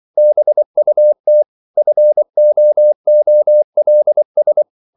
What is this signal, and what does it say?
Morse code, heard as a keyed tone, about 24 wpm, 600 Hz. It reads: BUT FOOLS
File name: CW / Practice / EnglishQuotations0224